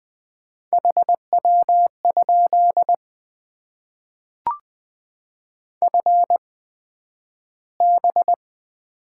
Morse code, heard as a keyed tone, about 20 wpm, spaced as for 11 wpm, 700 Hz. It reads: HW? FB
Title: Top 20 QSO Elements - 8x Character Spacing - 20wpm